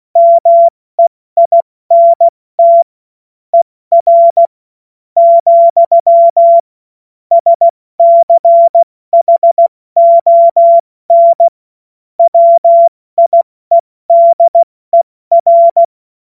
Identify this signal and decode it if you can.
Morse code, heard as a keyed tone, about 16 wpm, spaced as for 13 wpm, 680 Hz. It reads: MEINT ER, SCHON WIEDER
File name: Morsekurs / Witze / Witz 01. November 2022 70 F80